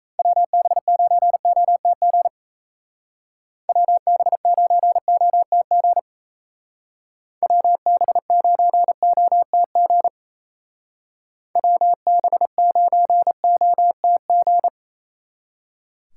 Morse code, heard as a keyed tone, about 33 wpm, 700 Hz. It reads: WB9OTG WB9OTG WB9OTG WB9OTG
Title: Speed Racing - USA 2x3 Call Signs 28wpm